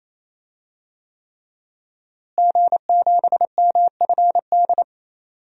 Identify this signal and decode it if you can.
Morse code, heard as a keyed tone, about 28 wpm, 700 Hz. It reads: G7MFD
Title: DXCC-100 Call Signs - 5 Characters - Repeated 3x - 28wpm